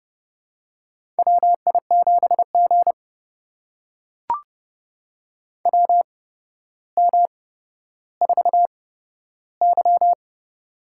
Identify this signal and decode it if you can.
Morse code, heard as a keyed tone, about 30 wpm, spaced as for 13 wpm, 700 Hz. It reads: WI7G WM4Y
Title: USA Call Signs - 4 Characters - 8x Character Spacing - 30wpm